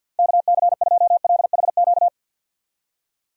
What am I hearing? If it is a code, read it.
Morse code, heard as a keyed tone, about 49 wpm, 700 Hz. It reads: KC2LHQ
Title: USA Call Signs - 6 Characters - 50wpm